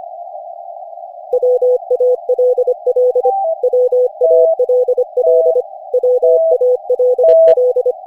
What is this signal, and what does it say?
Morse code, heard as a keyed tone, about 25 wpm, 500 Hz. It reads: WALL WALL WALL
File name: cw / Session5 cw training